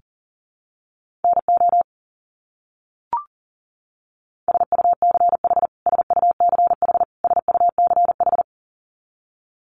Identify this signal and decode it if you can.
Morse code, heard as a keyed tone, about 40 wpm, 700 Hz. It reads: NO SUCH SUCH SUCH
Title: Top 200 Words Repeated 40wpm